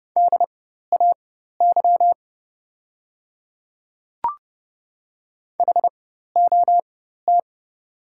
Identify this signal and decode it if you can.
Morse code, heard as a keyed tone, about 30 wpm, spaced as for 14 wpm, 700 Hz. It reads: DAY HOT
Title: 4x Character Spacing - Top 200 Words - 30wpm